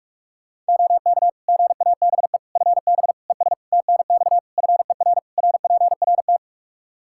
Morse code, heard as a keyed tone, about 46 wpm, 700 Hz. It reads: OK GABE FB ES TNX FER RPRT
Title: Practice QSO - Exchange 2 - Info - 45wpm